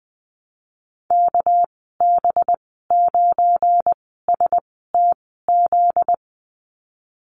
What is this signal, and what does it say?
Morse code, heard as a keyed tone, about 20 wpm, spaced as for 15 wpm, 700 Hz. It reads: KB9STZ
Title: 245 Lesson 39g - Letter z - Call Signs 20wpm